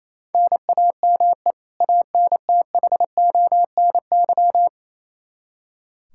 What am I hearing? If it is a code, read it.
Morse code, heard as a keyed tone, about 27 wpm, 700 Hz. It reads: NAME ANTHONY
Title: K1USN SST QSO Practice 28wpm